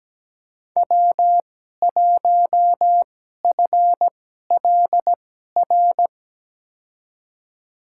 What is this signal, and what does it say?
Morse code, heard as a keyed tone, about 17 wpm, spaced as for 12 wpm, 700 Hz. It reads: W1FLR
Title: USA Call Signs - 5 Characters - 2x Character Spacing - 17wpm